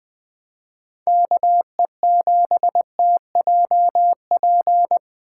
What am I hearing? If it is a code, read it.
Morse code, heard as a keyed tone, about 20 wpm, 700 Hz. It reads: KE7TJP